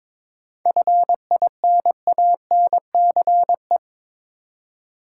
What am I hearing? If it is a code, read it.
Morse code, heard as a keyed tone, about 22 wpm, 700 Hz. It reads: FINANCE
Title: Seven Letter Words - 22wpm